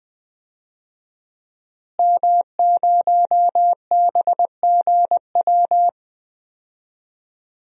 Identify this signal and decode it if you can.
Morse code, heard as a keyed tone, about 20 wpm, 700 Hz. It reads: M0BGW
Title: DXCC-100 Call Signs - 5 Characters - Repeated 2x - 20wpm